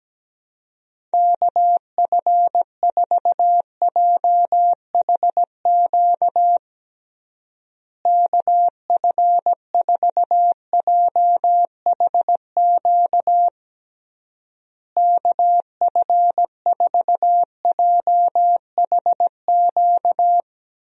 Morse code, heard as a keyed tone, about 17 wpm, 700 Hz. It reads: KF4JHQ KF4JHQ KF4JHQ
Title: USA Call Signs - 6 Characters - Repeated 3x - 17wpm